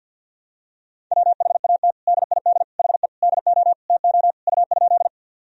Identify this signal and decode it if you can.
Morse code, heard as a keyed tone, about 47 wpm, 700 Hz. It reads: WHAT DID HE DO TO U?
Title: Sentences from Top 100 Words and Abbreviations 50wpm